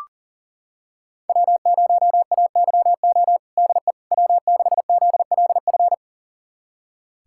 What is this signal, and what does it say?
Morse code, heard as a keyed tone, about 40 wpm, 700 Hz. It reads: W0AYO DE W6ZLF
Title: USA Callsign DE Callsign - 5 Characters - 40wpm